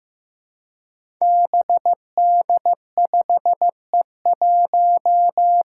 Morse code, heard as a keyed tone, about 15 wpm, 700 Hz. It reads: BD5E1
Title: Five Challenging Letter-Number Combinations 15wpm